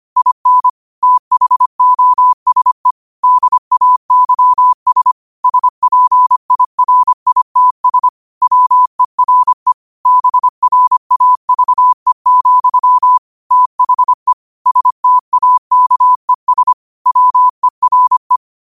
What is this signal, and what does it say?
Morse code, heard as a keyed tone, about 25 wpm, 1000 Hz. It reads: IN THOSE DAYS SPIRITS WERE BRAVE, THE STAKES WERE
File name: qotdfiles / 25wpm